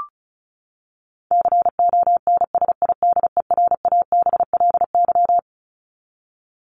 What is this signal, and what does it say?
Morse code, heard as a keyed tone, about 34 wpm, 700 Hz. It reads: CONSIDERABLY
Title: LY Suffix - Top 100 Words with Base Words 35wpm